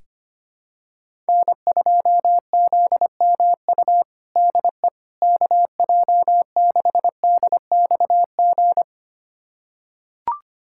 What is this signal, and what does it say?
Morse code, heard as a keyed tone, about 25 wpm, 700 Hz. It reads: N2ZMU DE KJ6DXG
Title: USA Callsign DE Callsign 25wpm